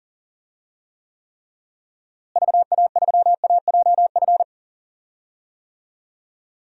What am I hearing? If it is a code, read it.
Morse code, heard as a keyed tone, about 40 wpm, 700 Hz. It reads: VA3AJF
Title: DXCC-100 Call Signs - 6 Characters - Repeated 3x - 40wpm